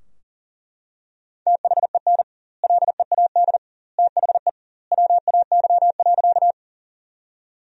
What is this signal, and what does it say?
Morse code, heard as a keyed tone, about 40 wpm, 700 Hz. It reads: THEN LEAD THE WAY.
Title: 2x Word-Spacing - Sentences from Top 500 Words - 40wpm